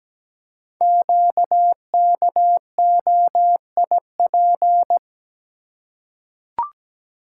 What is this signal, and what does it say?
Morse code, heard as a keyed tone, about 17 wpm, 700 Hz. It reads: QKOIP
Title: Code Groups - 5 Letters - 17wpm